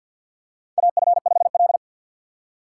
Morse code, heard as a keyed tone, about 53 wpm, 700 Hz. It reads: AF5L
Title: USA Call Signs - 4 Characters - 3x Character Spacing - 50wpm